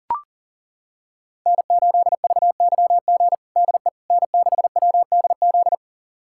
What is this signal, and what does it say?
Morse code, heard as a keyed tone, about 40 wpm, 700 Hz. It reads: N8VYG DE N6WDZ
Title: USA Callsign DE Callsign - 5 Characters - 40wpm